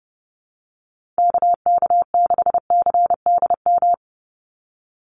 Morse code, heard as a keyed tone, about 30 wpm, 700 Hz. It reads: KK6CDM
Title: USA Call Signs 30wpm